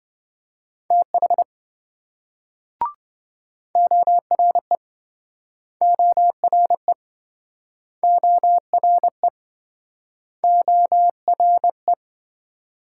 Morse code, heard as a keyed tone, about 27 wpm, 700 Hz. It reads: TH ORE ORE ORE ORE